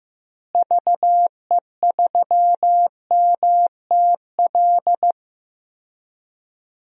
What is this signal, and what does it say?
Morse code, heard as a keyed tone, about 15 wpm, 700 Hz. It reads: VE3MTL
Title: DXCC-100 Call Signs - 6 Characters - Repeated 2x - 15wpm